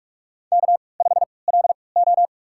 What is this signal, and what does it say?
Morse code, heard as a keyed tone, about 44 wpm, spaced as for 25 wpm, 700 Hz. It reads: K5LO
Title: USA Call Signs - 3x Character Spacing - 45wpm